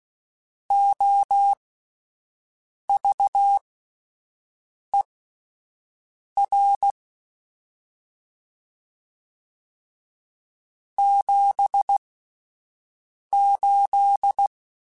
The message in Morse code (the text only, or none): OVER 78